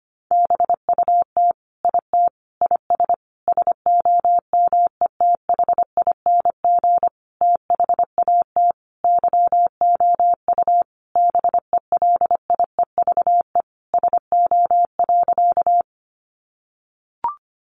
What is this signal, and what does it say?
Morse code, heard as a keyed tone, about 25 wpm, 700 Hz. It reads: BUT IT IS SOMETHING THAT YOU BELIEVE SO.
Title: Sentences from Top 1000 Words 25wpm